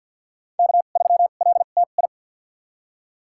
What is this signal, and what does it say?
Morse code, heard as a keyed tone, about 52 wpm, spaced as for 36 wpm, 700 Hz. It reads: K3LTI
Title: USA Call Signs - 2x Character Spacing - 50wpm